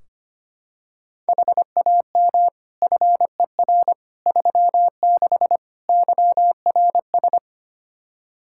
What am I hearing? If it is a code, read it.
Morse code, heard as a keyed tone, about 25 wpm, 700 Hz. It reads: HAM FER 36 YRS
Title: K1USN SST QSO Practice 25wpm